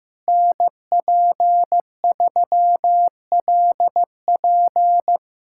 Morse code, heard as a keyed tone, about 16 wpm, 700 Hz. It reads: NP3LP
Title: USA Callsign DE Callsign - 5 Characters - 15wpm